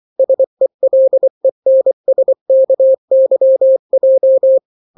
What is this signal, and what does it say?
Morse code, heard as a keyed tone, about 24 wpm, 530 Hz. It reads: SELENSKYJ
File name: daily / 23wpm